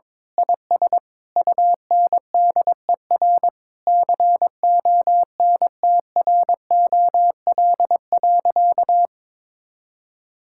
Morse code, heard as a keyed tone, about 22 wpm, 700 Hz. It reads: IS UNDER CONTROL.
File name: Sentences with 9-Word Phrases - 22wpm